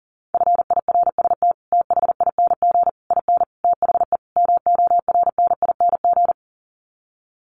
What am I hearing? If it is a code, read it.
Morse code, heard as a keyed tone, about 40 wpm, 700 Hz. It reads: FIRST THING IN THE MORNING
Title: Sets of 5 Words Encore2 40wpm